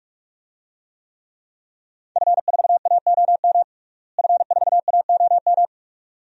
Morse code, heard as a keyed tone, about 43 wpm, 700 Hz. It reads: F4AOM F4AOM
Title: DXCC-100 Call Signs - 5 Characters - Repeated 2x - 45wpm